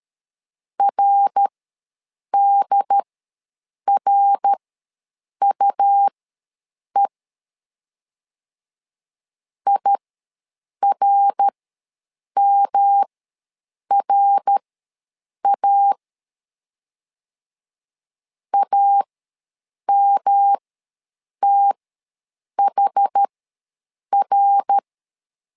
Morse code, heard as a keyed tone, about 13 wpm, spaced as for 7 wpm, 800 Hz. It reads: RDRUE IRMRA AMTHR
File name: LEZIONI / 3 MWRO / 3 MWR ws 12wpm 1809